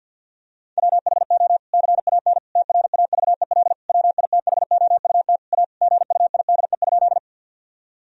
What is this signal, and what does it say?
Morse code, heard as a keyed tone, about 50 wpm, 700 Hz. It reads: WHO CAN TRAVEL WITHOUT A GUIDE?
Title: Sentences from Top 1000 Words - Review Part 3 - 50wpm